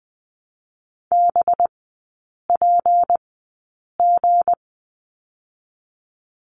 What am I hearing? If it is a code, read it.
Morse code, heard as a keyed tone, about 20 wpm, 700 Hz. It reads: B P G
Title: 184 Lesson 30e - Letter b - Three Characters 20wpm